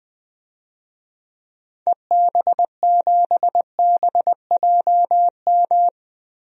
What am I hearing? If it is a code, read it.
Morse code, heard as a keyed tone, about 20 wpm, 700 Hz. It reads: EB7BJM
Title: DXCC-100 Call Signs - 6 Characters - Repeated 3x - 20wpm